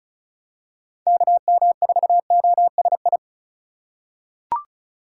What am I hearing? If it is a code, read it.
Morse code, heard as a keyed tone, about 36 wpm, 700 Hz. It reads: KM4OSI
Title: USA Call Signs - 6 Characters - 3x Character Spacing - 35wpm